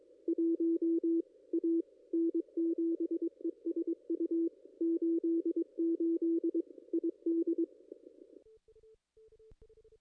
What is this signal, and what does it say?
Morse code, heard as a keyed tone, about 22 wpm, 330 Hz, with radio noise. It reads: 1 A N7ESU 88 ID
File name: contests / ss-cw-k5tr-op-n5rz -2200